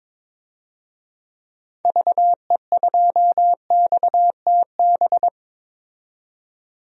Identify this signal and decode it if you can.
Morse code, heard as a keyed tone, about 22 wpm, 700 Hz. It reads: VE2XTB
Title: DXCC-100 Call Signs - 6 Characters - 22wpm